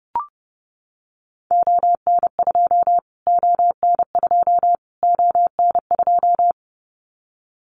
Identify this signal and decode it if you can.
Morse code, heard as a keyed tone, about 30 wpm, 700 Hz. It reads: ON2 ON2 ON2
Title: Callsign Prefixes for Top 25 DXCC - Repeated 3x - 30wpm